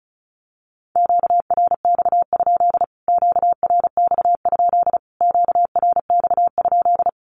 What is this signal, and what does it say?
Morse code, heard as a keyed tone, about 34 wpm, 700 Hz. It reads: QRX? QRX? QRX?
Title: Q-Codes Repeated 3x 35wpm